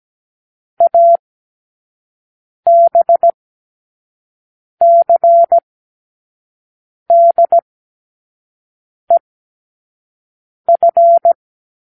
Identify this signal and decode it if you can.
Morse code, heard as a keyed tone, about 17 wpm, spaced as for 5 wpm, 680 Hz. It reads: ABCDEF
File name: k7qo / Lesson9(A-F)